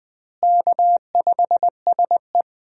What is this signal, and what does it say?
Morse code, heard as a keyed tone, about 20 wpm, 700 Hz. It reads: K5SE
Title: USA Call Signs - 4 Characters - Repeated 3x - 20wpm